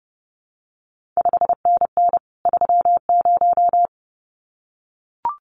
Morse code, heard as a keyed tone, about 30 wpm, 700 Hz. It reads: 5NN 30
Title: CQ WW DX Contest Exchange Encore - 30wpm